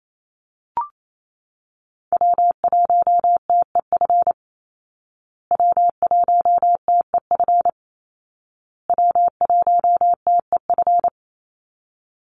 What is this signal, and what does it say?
Morse code, heard as a keyed tone, about 28 wpm, 700 Hz. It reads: W1TEF W1TEF W1TEF
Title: Call Signs Repeated 2x - 28wpm